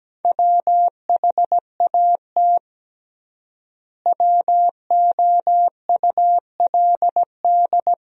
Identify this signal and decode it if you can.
Morse code, heard as a keyed tone, about 17 wpm, 700 Hz. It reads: WHAT WOULD
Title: 3x Word-Spacing - Sentences from Top 300 Words - 17wpm